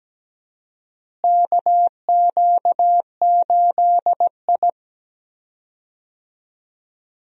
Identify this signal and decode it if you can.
Morse code, heard as a keyed tone, about 17 wpm, 700 Hz. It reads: KQ8I
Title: USA Call Signs - 4 Characters - Repeated 3x - 17wpm